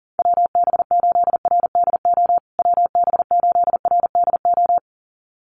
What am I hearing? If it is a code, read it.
Morse code, heard as a keyed tone, about 40 wpm, 700 Hz. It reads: WB8RDO WB8RDO
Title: USA Call Signs Repeated 2x 40wpm